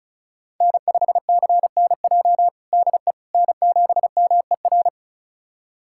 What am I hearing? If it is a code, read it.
Morse code, heard as a keyed tone, about 34 wpm, 700 Hz. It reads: N5CNJ DE N7MER
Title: USA Callsign DE Callsign - 5 Characters - 35wpm